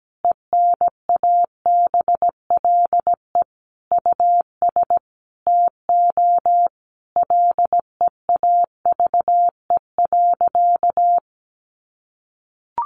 Morse code, heard as a keyed tone, about 17 wpm, 700 Hz. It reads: ENABLE US TO LEAVE.
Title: Sentences from Top 1500 Words 17wpm